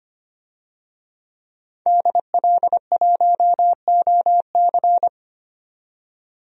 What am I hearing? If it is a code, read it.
Morse code, heard as a keyed tone, about 25 wpm, 700 Hz. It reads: DL1OC
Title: DXCC-100 Call Signs - 5 Characters - 25wpm